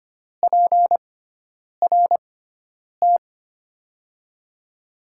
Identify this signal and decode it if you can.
Morse code, heard as a keyed tone, about 25 wpm, 700 Hz. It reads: P R T
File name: Top 90 QSO Elements - 6x Character Spacing - 25wpm